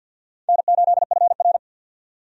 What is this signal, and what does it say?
Morse code, heard as a keyed tone, about 47 wpm, 700 Hz. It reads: N8FR
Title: USA Call Signs - 4 Characters - 50wpm